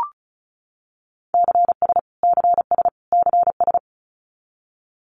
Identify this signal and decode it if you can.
Morse code, heard as a keyed tone, about 34 wpm, 700 Hz. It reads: CS CS CS
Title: Abbreviations Repeated 3x - 35wpm